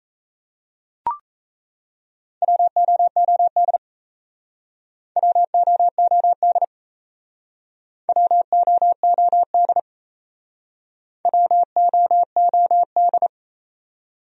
Morse code, heard as a keyed tone, about 38 wpm, 700 Hz. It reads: WOOD WOOD WOOD WOOD